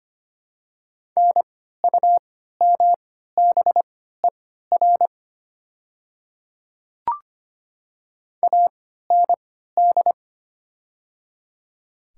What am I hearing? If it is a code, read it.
Morse code, heard as a keyed tone, about 25 wpm, spaced as for 14 wpm, 700 Hz. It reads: NUMBER AND
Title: 3x Character Spacing - Top 100 Words - 25wpm